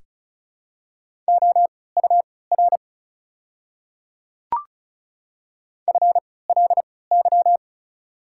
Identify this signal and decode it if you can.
Morse code, heard as a keyed tone, about 36 wpm, spaced as for 20 wpm, 700 Hz. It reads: OUR FLY
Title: 3x Character Spacing - Top 500 Words - 35wpm